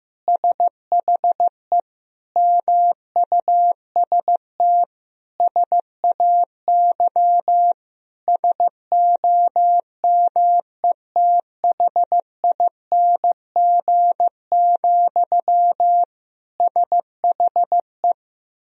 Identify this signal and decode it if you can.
Morse code, heard as a keyed tone, about 15 wpm, 700 Hz. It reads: SHE MUST SAY SOMETHING, SHE
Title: Sentences from Top 1000 Words - Review Part 3 - 15wpm